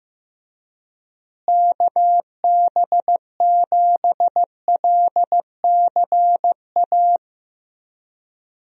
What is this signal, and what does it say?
Morse code, heard as a keyed tone, about 15 wpm, 700 Hz. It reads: KB7LCA
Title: USA Call Signs - 6 Characters - Repeated 2x - 15wpm